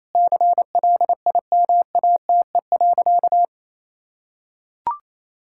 Morse code, heard as a keyed tone, about 28 wpm, 700 Hz. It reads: CLIMATE.
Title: Sentences from Top 1600 Words 28wpm